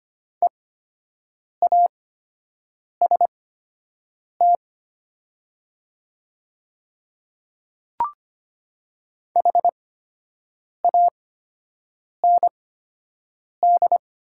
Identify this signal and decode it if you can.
Morse code, heard as a keyed tone, about 25 wpm, spaced as for 7 wpm, 700 Hz. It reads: EAST HAND
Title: 8x Character Spacing - Top 500 Words - 25wpm